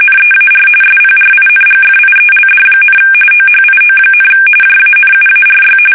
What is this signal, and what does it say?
RTTY, about 45.45 baud. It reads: NEAR SPRINGFIELD, OHIO, JANUARY
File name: fsk850 / benjamin-g-lamme